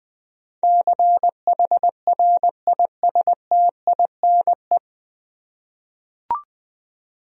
Morse code, heard as a keyed tone, about 20 wpm, 700 Hz. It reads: CHRISTINE
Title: USA Common Names - 20wpm